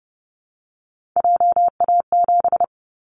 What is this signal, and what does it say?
Morse code, heard as a keyed tone, about 31 wpm, 700 Hz. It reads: JA7
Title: Callsign Prefixes for Top 50 DXCC - 30wpm